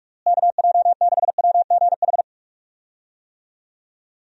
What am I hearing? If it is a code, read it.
Morse code, heard as a keyed tone, about 41 wpm, 700 Hz. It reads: KJ6WGH
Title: USA Call Signs - 6 Characters - Repeated 2x - 45wpm